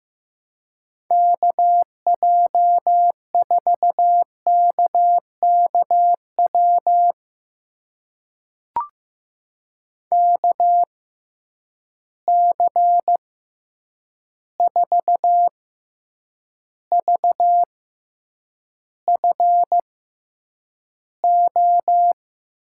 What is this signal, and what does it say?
Morse code, heard as a keyed tone, about 15 wpm, spaced as for 8 wpm, 700 Hz. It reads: KJ4KKW KC4VFO
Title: USA Call Signs - 6 Characters - 6x Character Spacing - 15wpm